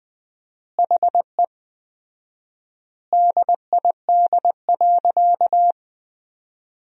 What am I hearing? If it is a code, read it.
Morse code, heard as a keyed tone, about 21 wpm, 700 Hz. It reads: HE DID.
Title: 4x Word-Spacing - Sentences from Top 500 Words - 20wpm